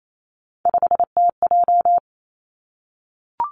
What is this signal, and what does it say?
Morse code, heard as a keyed tone, about 28 wpm, 700 Hz. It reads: 5TJ
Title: Three Challenging Letter-Number Combinations 28wpm